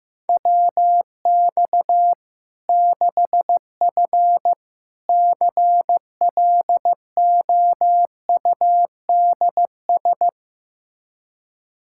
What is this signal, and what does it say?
Morse code, heard as a keyed tone, about 16 wpm, 700 Hz. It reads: WX 6F CLOUDS
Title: Practice QSO - Exchange 2 - Info - 15wpm